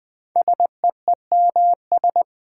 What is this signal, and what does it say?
Morse code, heard as a keyed tone, about 21 wpm, 700 Hz. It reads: SEEMS
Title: 4x Word-Spacing - Sets of 3 Words - 20wpm